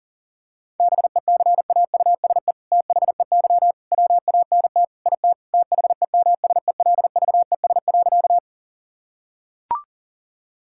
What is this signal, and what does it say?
Morse code, heard as a keyed tone, about 40 wpm, 700 Hz. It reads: BECAUSE THEY WANT IT THEMSELVES.
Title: Sentences from Top 1000 Words - Review Part 5 - 40wpm